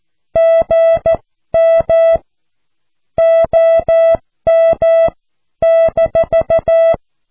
Morse code, heard as a keyed tone, about 14 wpm, spaced as for 12 wpm, 660 Hz. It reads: GM OM -
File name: cw / QSO